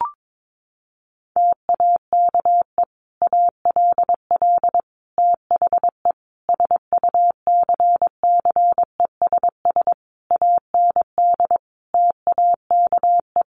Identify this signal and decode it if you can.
Morse code, heard as a keyed tone, about 22 wpm, 700 Hz. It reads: TAKE ALL THE SUCCESS AND TAKE
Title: Sentences from Top 900 Words 22wpm